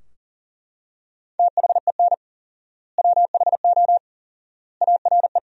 Morse code, heard as a keyed tone, about 40 wpm, 700 Hz. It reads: THEN WHO ARE
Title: 4x Word-Spacing - Sentences from Top 100 Words - 40wpm